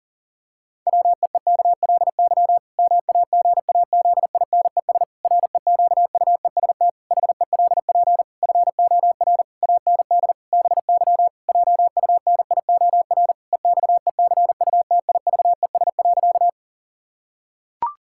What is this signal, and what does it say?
Morse code, heard as a keyed tone, about 40 wpm, 700 Hz. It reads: WEEKLY MAGAZINES REQUEST HELP FOR AND BY JUNIOR EXECUTIVES.
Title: Pangram Challenge v2 - 40wpm